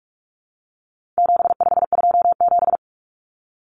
Morse code, heard as a keyed tone, about 44 wpm, 700 Hz. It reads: 7527
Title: Numbers - 4-Digit 45wpm